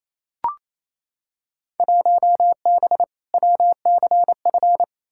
Morse code, heard as a keyed tone, about 28 wpm, 700 Hz. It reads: 1B WCF
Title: Field Day Exchanges Encore - 28wpm